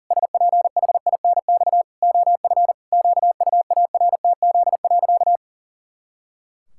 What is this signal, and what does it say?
Morse code, heard as a keyed tone, about 40 wpm, 700 Hz. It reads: SPHINX OF QUARTZ.
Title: Pangram Challenge v2 - 40wpm